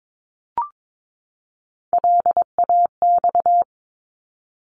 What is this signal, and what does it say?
Morse code, heard as a keyed tone, about 22 wpm, 700 Hz. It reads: LAX
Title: ARRL Sections 22wpm